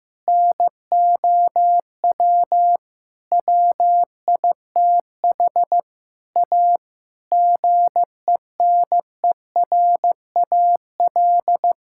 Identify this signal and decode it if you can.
Morse code, heard as a keyed tone, about 16 wpm, 700 Hz. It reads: NOW WITH A GENERAL
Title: Sentences from Top 1600 Words 15wpm